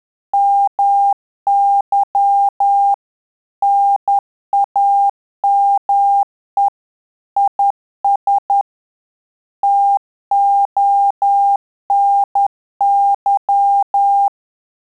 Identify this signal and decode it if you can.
Morse code, heard as a keyed tone, about 10 wpm, 800 Hz. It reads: MY NAME IS TONY
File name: Telegraphy-group / QSO CW 10 PPM